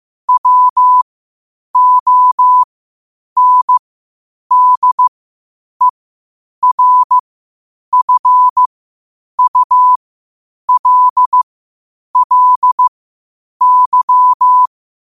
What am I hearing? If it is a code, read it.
Morse code, heard as a keyed tone, about 15 wpm, spaced as for 8 wpm, 1000 Hz. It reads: WONDERFULLY
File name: qotdfiles / 8wpm